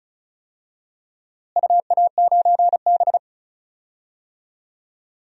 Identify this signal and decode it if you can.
Morse code, heard as a keyed tone, about 36 wpm, 690 Hz. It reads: UA9B
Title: DXCC-100 Call Signs - 4 Characters - Repeated 3x - 35wpm